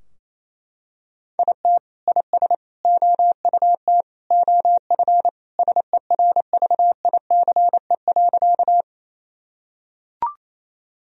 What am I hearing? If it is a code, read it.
Morse code, heard as a keyed tone, about 28 wpm, 700 Hz. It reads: IT IS OUT OF SERVICE.
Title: Sentences from Top 1000 Words - Review Part 4 - 28wpm